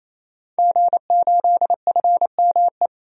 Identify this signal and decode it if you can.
Morse code, heard as a keyed tone, about 28 wpm, 700 Hz. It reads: G8FME
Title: DXCC-100 Call Signs - 5 Characters - 28wpm